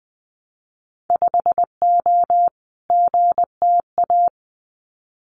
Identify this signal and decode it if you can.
Morse code, heard as a keyed tone, about 20 wpm, 700 Hz. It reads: 5O GTA
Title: Winter Field Day Exchanges Encore 20wpm